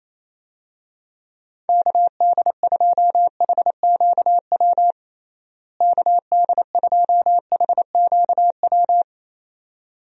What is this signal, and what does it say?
Morse code, heard as a keyed tone, about 28 wpm, 700 Hz. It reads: KD2HQW KD2HQW
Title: USA Call Signs - 6 Characters - Repeated 2x - 28wpm